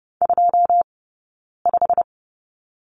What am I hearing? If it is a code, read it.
Morse code, heard as a keyed tone, about 30 wpm, 700 Hz. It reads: 2 5
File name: course / 30:10wpm / 099 Lesson 17d - Letter u - Two Characters 30wpm